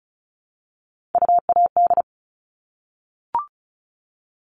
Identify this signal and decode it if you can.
Morse code, heard as a keyed tone, about 34 wpm, 700 Hz. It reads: UAD
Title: All Cut Numbers - 3-Digit 35wpm